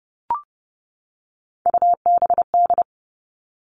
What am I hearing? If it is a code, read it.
Morse code, heard as a keyed tone, about 30 wpm, 700 Hz. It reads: UBD
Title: All Cut Numbers - 3-Digit 30wpm